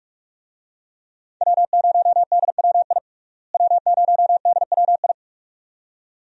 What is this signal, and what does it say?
Morse code, heard as a keyed tone, about 44 wpm, 700 Hz. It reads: W0DWI W0DWI
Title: USA Call Signs - 5 Characters - Repeated 2x - 45wpm